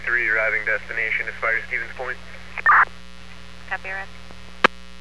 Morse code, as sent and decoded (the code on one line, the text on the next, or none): none